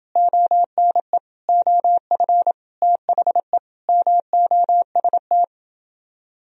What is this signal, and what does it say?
Morse code, heard as a keyed tone, about 29 wpm, 700 Hz. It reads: ONE OF THE MOST